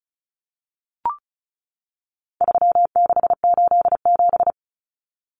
Morse code, heard as a keyed tone, about 34 wpm, 700 Hz. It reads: 3687
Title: Numbers - 4-Digit 35wpm